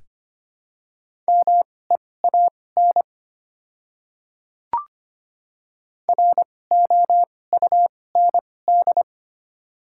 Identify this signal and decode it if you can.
Morse code, heard as a keyed tone, about 25 wpm, spaced as for 18 wpm, 700 Hz. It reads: MEAN ROUND
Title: 2x Character Spacing - Top 200 Words - 25wpm